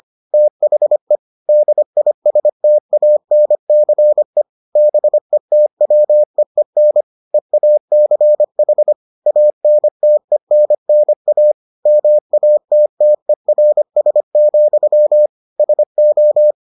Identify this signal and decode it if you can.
Morse code, heard as a keyed tone, about 25 wpm, 600 Hz. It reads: THE DISTANCE BETWEEN EACH ANTENNA MATTERS, SO